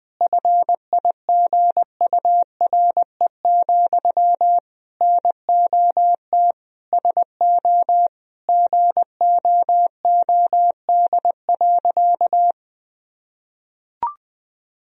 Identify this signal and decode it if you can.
Morse code, heard as a keyed tone, about 20 wpm, 700 Hz. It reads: FIGURE, NOT SO GOOD.
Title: Sentences from Top 1000 Words - Review Part 2 - 20wpm